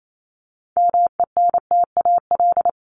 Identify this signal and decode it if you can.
Morse code, heard as a keyed tone, about 28 wpm, 700 Hz. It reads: MENTAL
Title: Top 501-750 Words 28wpm